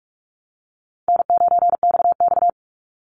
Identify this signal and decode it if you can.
Morse code, heard as a keyed tone, about 44 wpm, 700 Hz. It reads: N9XX
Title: Easy Call Signs Repeated 2x - 45wpm